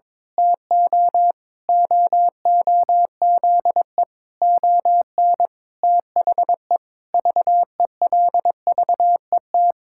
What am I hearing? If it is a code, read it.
Morse code, heard as a keyed tone, about 22 wpm, 700 Hz. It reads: TO OOZE ON THE VELVET
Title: Pangram Challenge v2 - 22wpm